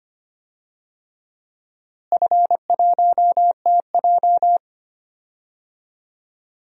Morse code, heard as a keyed tone, about 25 wpm, 700 Hz. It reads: F1TJ
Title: DXCC-100 Call Signs - 4 Characters - Repeated 3x - 25wpm